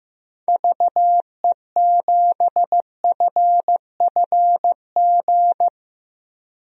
Morse code, heard as a keyed tone, about 16 wpm, 700 Hz. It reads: VE7FFG
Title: DXCC-100 Call Signs - 6 Characters - Repeated 2x - 15wpm